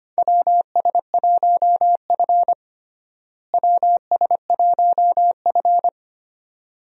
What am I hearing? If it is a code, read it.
Morse code, heard as a keyed tone, about 25 wpm, 700 Hz. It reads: WS1F WS1F
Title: USA Call Signs - 4 Characters - Repeated 2x - 25wpm